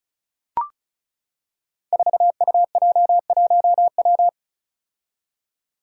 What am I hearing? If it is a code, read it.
Morse code, heard as a keyed tone, about 36 wpm, 700 Hz. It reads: 4UJ1W
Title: Five Challenging Letter-Number Combinations 35wpm